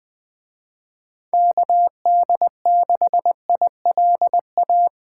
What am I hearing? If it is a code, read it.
Morse code, heard as a keyed tone, about 20 wpm, 700 Hz. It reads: KD6ILA
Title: Speed Racing - USA 2x3 Call Signs 17wpm